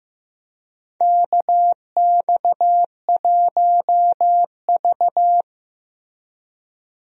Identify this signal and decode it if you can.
Morse code, heard as a keyed tone, about 15 wpm, 700 Hz. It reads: KX1V
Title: USA Call Signs - 4 Characters - Repeated 3x - 15wpm